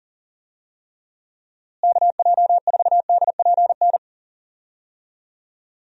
Morse code, heard as a keyed tone, about 40 wpm, 700 Hz. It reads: KJ4DPN
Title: USA Call Signs - 6 Characters - Repeated 3x - 40wpm